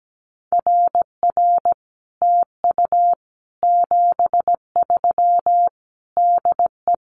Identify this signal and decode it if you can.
Morse code, heard as a keyed tone, about 17 wpm, 700 Hz. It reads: RR TU 73 DE
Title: Common QSO Phrases Encore 17wpm